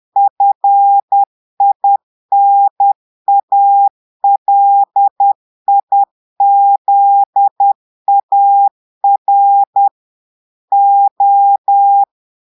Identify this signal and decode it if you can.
Morse code, heard as a keyed tone, about 10 wpm, 800 Hz. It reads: FINALIZAR O
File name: portugais texte en clair 2 vitesse 10 mots minutes